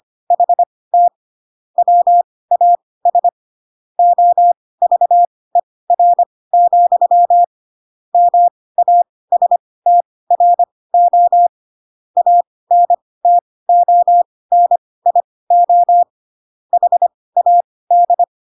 Morse code, heard as a keyed tone, about 25 wpm, spaced as for 18 wpm, 700 Hz. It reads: HT WAS OVER, MASTRO ANTONIO HAD
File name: Pinocchio / Pinocchio ch2 12 25 ik1hge